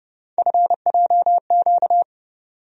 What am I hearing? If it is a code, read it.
Morse code, heard as a keyed tone, about 30 wpm, 700 Hz. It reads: FJQ
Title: USA Call Sign Postfixes - 3 Letters 30wpm